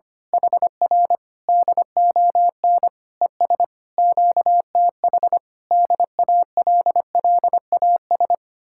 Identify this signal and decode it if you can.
Morse code, heard as a keyed tone, about 24 wpm, 700 Hz. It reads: HR DON ES QTH DALLAS
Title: Speed-Racing Common QSO Phrases - 25wpm